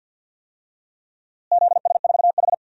Morse code, heard as a keyed tone, about 53 wpm, 700 Hz. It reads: ZS4H
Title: DXCC-100 Call Signs - 4 Characters - Repeated 3x - 50wpm